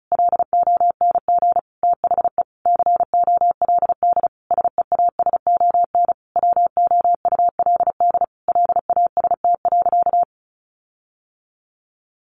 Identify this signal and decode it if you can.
Morse code, heard as a keyed tone, about 34 wpm, 700 Hz. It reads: LONG THE COLD SEASON WOULD LAST.
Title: Sentences from Top 1400 Words 35wpm